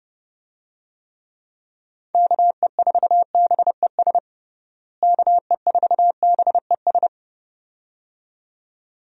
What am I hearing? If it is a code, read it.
Morse code, heard as a keyed tone, about 30 wpm, 700 Hz. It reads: KE4BES KE4BES
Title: USA Call Signs - 6 Characters - Repeated 2x - 30wpm